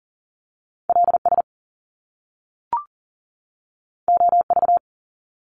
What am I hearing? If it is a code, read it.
Morse code, heard as a keyed tone, about 42 wpm, 700 Hz. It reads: LS OV
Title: Top Two Letter Combinations in Words 40wpm